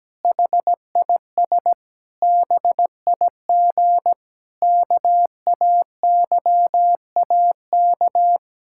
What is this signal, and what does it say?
Morse code, heard as a keyed tone, about 17 wpm, 700 Hz. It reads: HIS BIG KAYAK
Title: Pangram Challenge v2 - 17wpm